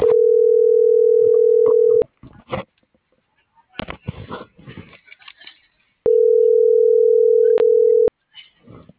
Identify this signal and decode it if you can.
Ringback tone (North America), 440 + 480 Hz.
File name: CENTRAL GOMSA / out aylin.bazan